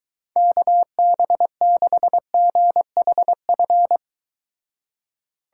Morse code, heard as a keyed tone, about 23 wpm, 700 Hz. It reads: KB6GHF